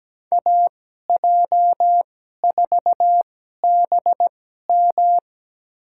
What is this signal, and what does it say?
Morse code, heard as a keyed tone, about 17 wpm, spaced as for 12 wpm, 700 Hz. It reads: AJ4BM
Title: USA Call Signs - 5 Characters - 2x Character Spacing - 17wpm